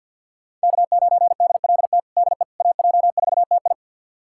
Morse code, heard as a keyed tone, about 51 wpm, 700 Hz. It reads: K9DLT DE AJ4TI
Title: USA Callsign DE Callsign - 5 Characters - 50wpm